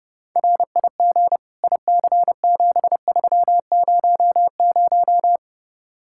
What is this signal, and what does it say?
Morse code, heard as a keyed tone, about 35 wpm, 700 Hz. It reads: RIG IC7300
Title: Speed-Racing Common QSO Phrases - 30wpm